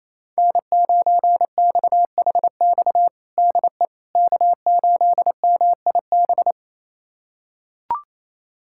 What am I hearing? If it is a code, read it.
Morse code, heard as a keyed tone, about 28 wpm, 700 Hz. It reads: N9XHX DE K8MIB
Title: USA Callsign DE Callsign - 5 Characters - 28wpm